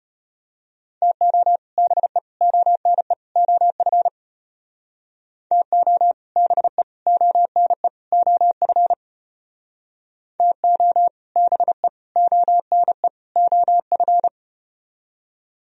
Morse code, heard as a keyed tone, about 35 wpm, 700 Hz. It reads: TO BE ONE OF TO BE ONE OF TO BE ONE OF